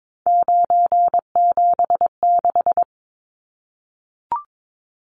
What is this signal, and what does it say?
Morse code, heard as a keyed tone, about 22 wpm, 700 Hz. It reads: 976
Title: Numbers - 3-Digit 22wpm